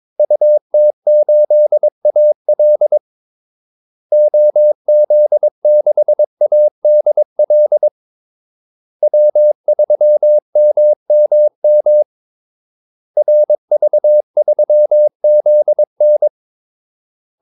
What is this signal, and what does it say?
Morse code, heard as a keyed tone, about 22 wpm, 600 Hz. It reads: UT8AL OZ6ADL W3MMM RV3ZN
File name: Callsigns-22wpm